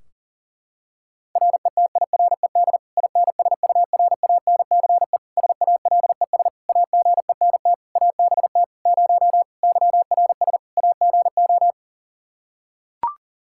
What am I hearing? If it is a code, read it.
Morse code, heard as a keyed tone, about 40 wpm, 700 Hz. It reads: RETIRED INSURANCE SALES AGENT ABT 0 YRS AGO
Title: Practice QSO - Exchange 3 - More Info - 40wpm